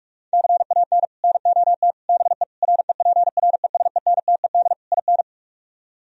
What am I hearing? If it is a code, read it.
Morse code, heard as a keyed tone, about 49 wpm, 700 Hz. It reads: CAN NOT BE REPRESENTED IN